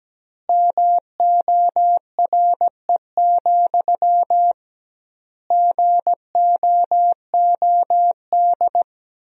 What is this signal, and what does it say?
Morse code, heard as a keyed tone, about 17 wpm, 700 Hz. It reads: MORE, GOOD
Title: 2x Word-Spacing - Sentences from Top 300 Words - 17wpm